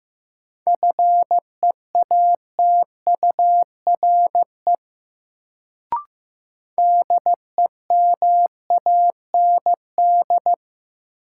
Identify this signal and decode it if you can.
Morse code, heard as a keyed tone, about 15 wpm, 700 Hz. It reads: FEATURE DEMAND
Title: Sentences from Top 800 Words - New Word Drill - 15wpm